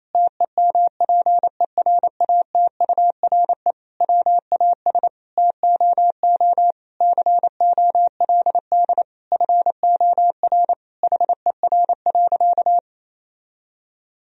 Morse code, heard as a keyed tone, about 28 wpm, 700 Hz. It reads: TEMPERATURE WAS TOO COLD FOR HER.
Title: Sentences from Top 1000 Words - Review Part 4 - 28wpm